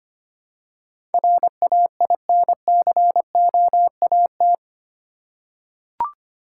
Morse code, heard as a keyed tone, about 25 wpm, 690 Hz. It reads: RAINCOAT
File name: practice / 25wpm / Easy Words with Hints - 25wpm